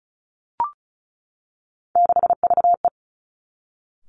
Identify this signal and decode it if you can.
Morse code, heard as a keyed tone, about 36 wpm, 700 Hz. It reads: 6VE
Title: Three Challenging Letter-Number Combinations 35wpm